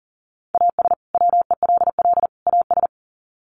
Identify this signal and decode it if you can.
Morse code, heard as a keyed tone, about 40 wpm, 700 Hz. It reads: AS WELL AS
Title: Sets of 3 Words 40wpm